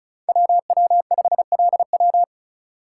Morse code, heard as a keyed tone, about 36 wpm, 700 Hz. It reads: WW5LW
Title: USA Call Signs - 3x Character Spacing - 35wpm